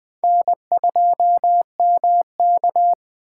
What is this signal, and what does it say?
Morse code, heard as a keyed tone, about 21 wpm, 700 Hz. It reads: N2MK
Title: Speed Racing - USA 1x2 Call Signs 17wpm